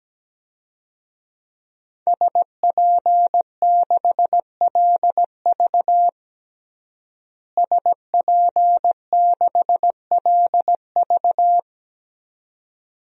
Morse code, heard as a keyed tone, about 17 wpm, 700 Hz. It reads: SP6LV SP6LV
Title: DXCC-100 Call Signs - 5 Characters - Repeated 2x - 17wpm